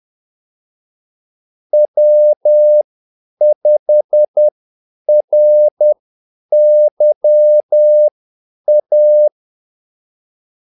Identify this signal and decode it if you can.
Morse code, heard as a keyed tone, about 10 wpm, spaced as for 8 wpm, 600 Hz. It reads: W5RYA
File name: Callsigns-10-8wpm